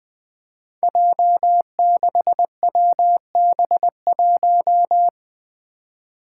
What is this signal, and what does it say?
Morse code, heard as a keyed tone, about 20 wpm, 700 Hz. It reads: J6WB1
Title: Five Challenging Letter-Number Combinations 20wpm